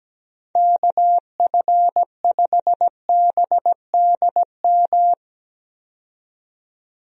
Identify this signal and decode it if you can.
Morse code, heard as a keyed tone, about 17 wpm, 700 Hz. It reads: KF5BDM
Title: USA Call Signs - 6 Characters - 17wpm